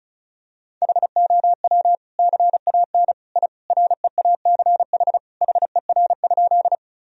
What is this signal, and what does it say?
Morse code, heard as a keyed tone, about 34 wpm, 700 Hz. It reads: HOW CAN I REACH HER?
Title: Sentences from Top 1000 Words - Review Part 2 - 35wpm